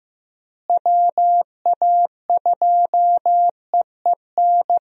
Morse code, heard as a keyed tone, about 15 wpm, 700 Hz. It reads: WA2EEN
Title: USA Call Signs - 6 Characters - Repeated 3x - 15wpm